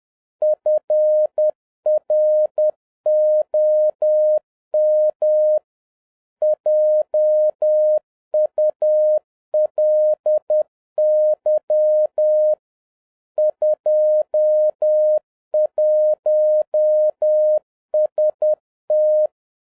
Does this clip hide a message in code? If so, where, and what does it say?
Yes: Morse code from 0.4 to 19.3 s, heard as a keyed tone, about 10 wpm, 610 Hz. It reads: FROM JULY 21ST